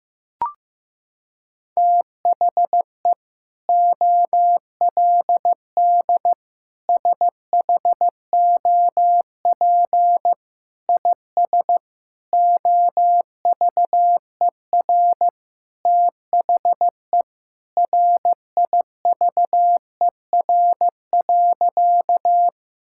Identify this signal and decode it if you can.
Morse code, heard as a keyed tone, about 14 wpm, 700 Hz. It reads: THE OLD SHOP IS OVER THE RIVER.
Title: Sentences from Top 1600 Words 15wpm